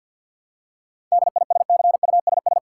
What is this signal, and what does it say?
Morse code, heard as a keyed tone, about 50 wpm, 700 Hz. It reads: DISCUSS
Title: Seven Letter Words - 50wpm